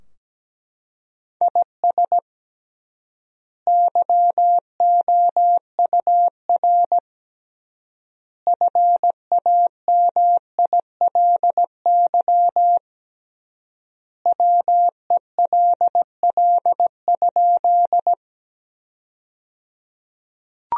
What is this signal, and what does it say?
Morse code, heard as a keyed tone, about 17 wpm, 700 Hz. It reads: IS YOUR FAMILY WELL?
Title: 3x Word-Spacing - Sentences from Top 400 Words - 17wpm